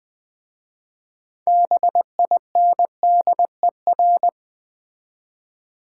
Morse code, heard as a keyed tone, about 20 wpm, 700 Hz. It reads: BINDER
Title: Difficult Words with Hints - 20wpm